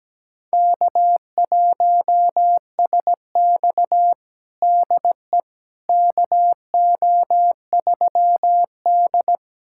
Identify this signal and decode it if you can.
Morse code, heard as a keyed tone, about 17 wpm, 700 Hz. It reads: K1SX DE KO3D
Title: USA Callsign DE Callsign - 4 Characters - 17wpm